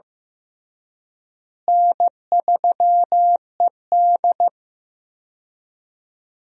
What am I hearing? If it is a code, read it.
Morse code, heard as a keyed tone, about 14 wpm, 700 Hz. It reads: N3ED
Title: USA Call Signs - 4 Characters - Repeated 3x - 15wpm